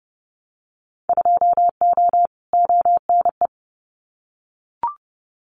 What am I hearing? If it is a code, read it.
Morse code, heard as a keyed tone, about 30 wpm, 700 Hz. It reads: 2O ONE
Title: Winter Field Day Exchanges Encore 30wpm